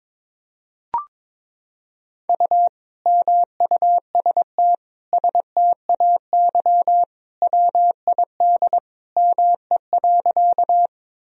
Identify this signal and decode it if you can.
Morse code, heard as a keyed tone, about 23 wpm, 700 Hz. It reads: U MUST STAY WID ME.
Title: Sentences from Top 500 Words and Abbreviations 22wpm